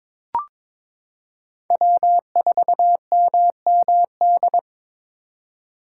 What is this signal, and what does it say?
Morse code, heard as a keyed tone, about 22 wpm, 700 Hz. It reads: W4MMD
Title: K1USN SST QSO Practice - 22wpm